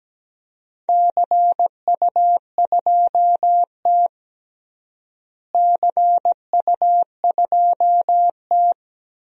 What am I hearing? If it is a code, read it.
Morse code, heard as a keyed tone, about 17 wpm, 700 Hz. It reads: CU2T CU2T
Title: DXCC-100 Call Signs - 4 Characters - Repeated 3x - 17wpm